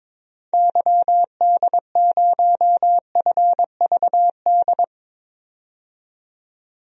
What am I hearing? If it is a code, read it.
Morse code, heard as a keyed tone, about 22 wpm, 700 Hz. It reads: YD0FVD
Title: DXCC-100 Call Signs - 6 Characters - 22wpm